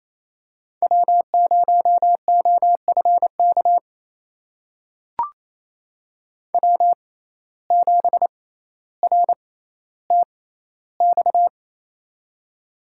Morse code, heard as a keyed tone, about 28 wpm, spaced as for 14 wpm, 700 Hz. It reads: W0OFK W7RTX
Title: USA Call Signs - 5 Characters - 6x Character Spacing - 28wpm